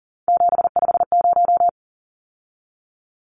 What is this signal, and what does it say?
Morse code, heard as a keyed tone, about 40 wpm, 700 Hz. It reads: 750
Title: Numbers - 3-Digit 40wpm